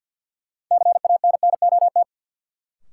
Morse code, heard as a keyed tone, about 49 wpm, 700 Hz. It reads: CANNOT
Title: Sentences from Top 1000 Words - Review Part 6 - New Word Drill - 50wpm